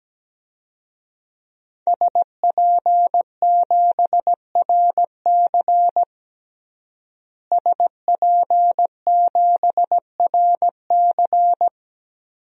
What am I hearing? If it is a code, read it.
Morse code, heard as a keyed tone, about 17 wpm, 700 Hz. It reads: SP7RC SP7RC
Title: DXCC-100 Call Signs - 5 Characters - Repeated 2x - 17wpm